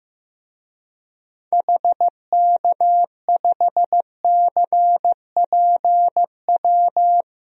Morse code, heard as a keyed tone, about 15 wpm, 700 Hz. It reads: HK5CPW
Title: DXCC-100 Call Signs - 6 Characters - 15wpm